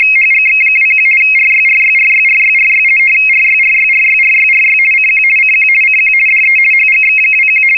RTTY, about 50 baud. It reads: HGX21 RYRYRYRYRY RYRYRYRYRY CQ CQ CQ DE HGX21
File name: europe / hng / hng-hngd-mfa budapest-rtty-50 bd-(1986)-(rb)-(mc-914-1-16)